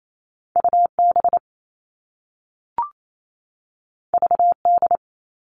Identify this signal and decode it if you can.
Morse code, heard as a keyed tone, about 28 wpm, 700 Hz. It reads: UB VD
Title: All Cut Numbers - 2-Digit 28wpm